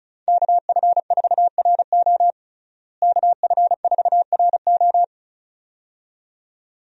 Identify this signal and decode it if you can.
Morse code, heard as a keyed tone, about 34 wpm, 700 Hz. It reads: KF4RO KF4RO
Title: USA Call Signs - 5 Characters - Repeated 2x - 35wpm